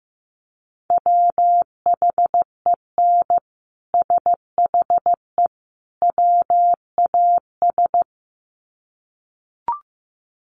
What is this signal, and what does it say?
Morse code, heard as a keyed tone, about 15 wpm, 700 Hz. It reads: WHEN SHE WAS
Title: Sets of 3 Words 15wpm